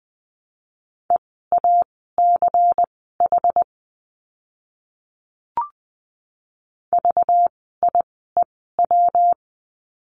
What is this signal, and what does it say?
Morse code, heard as a keyed tone, about 20 wpm, spaced as for 14 wpm, 700 Hz. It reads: EACH VIEW
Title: 192 Lesson 31f - Letter v - Words 20wpm